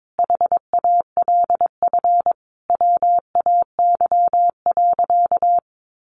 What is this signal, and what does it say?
Morse code, heard as a keyed tone, about 22 wpm, 700 Hz. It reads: HALF WAY.
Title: Sentences from Top 400 Words 22wpm